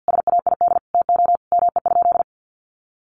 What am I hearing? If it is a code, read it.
Morse code, heard as a keyed tone, about 50 wpm, 700 Hz. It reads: SAID TO ME?
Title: Sentences from Top 200 Words 50wpm